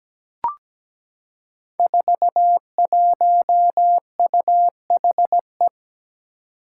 Morse code, heard as a keyed tone, about 17 wpm, 700 Hz. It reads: 41UHE
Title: Five Challenging Letter-Number Combinations 17wpm